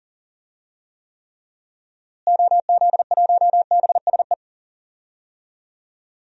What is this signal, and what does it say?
Morse code, heard as a keyed tone, about 39 wpm, 700 Hz. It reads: OZ1BSE
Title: DXCC-100 Call Signs - 6 Characters - Repeated 3x - 40wpm